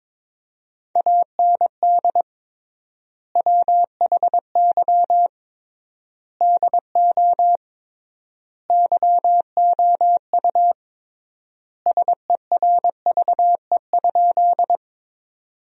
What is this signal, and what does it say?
Morse code, heard as a keyed tone, about 22 wpm, 700 Hz. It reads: AND WHY DO YOU SERVE?
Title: 3x Word-Spacing - Sentences from Top 400 Words - 22wpm